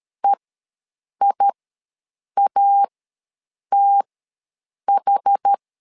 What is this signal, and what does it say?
Morse code, heard as a keyed tone, about 13 wpm, spaced as for 7 wpm, 800 Hz. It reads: EIATH
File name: LEZIONI / 2 UHDN / 2 UHD ws 12wpm 1722